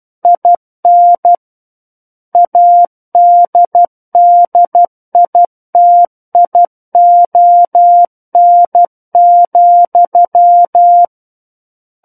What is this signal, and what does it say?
Morse code, heard as a keyed tone, about 12 wpm, 700 Hz. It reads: IN ADDITION,
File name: Book One / 03-Chapter 3 - 12 WPM